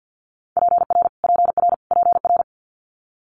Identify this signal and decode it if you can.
Morse code, heard as a keyed tone, about 50 wpm, 700 Hz. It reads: PR PR PR
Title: Multi-letter Phonemes Repeated 3x 50wpm